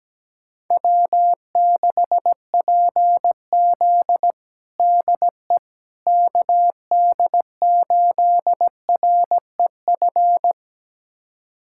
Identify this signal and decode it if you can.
Morse code, heard as a keyed tone, about 17 wpm, 700 Hz. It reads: W6PZ DE KD8REF
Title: USA Callsign DE Callsign 17wpm